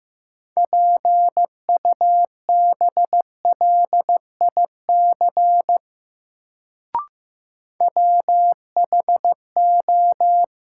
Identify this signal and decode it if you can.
Morse code, heard as a keyed tone, about 15 wpm, 700 Hz. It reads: PUBLIC WHO
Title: Sets of 2 Words - New Word Drill - 15wpm